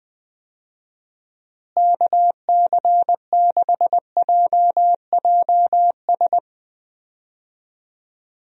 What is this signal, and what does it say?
Morse code, heard as a keyed tone, about 20 wpm, 700 Hz. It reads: KC6JJS
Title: USA Call Signs - 6 Characters - Repeated 2x - 20wpm